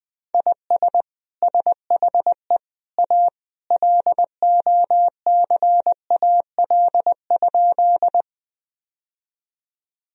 Morse code, heard as a keyed tone, about 20 wpm, 700 Hz. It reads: IS SHE A LOCAL?
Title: Sentences from Top 1000 Words - Review Part 3 - 20wpm